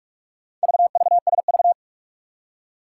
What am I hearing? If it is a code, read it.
Morse code, heard as a keyed tone, about 44 wpm, 700 Hz. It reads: VVSV
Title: Four Challenging Letter-Number Combinations 45wpm